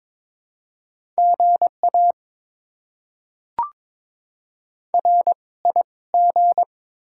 Morse code, heard as a keyed tone, about 22 wpm, spaced as for 17 wpm, 690 Hz. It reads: GA RIG
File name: Top 20 QSO Elements - 2x Character Spacing - 22wpm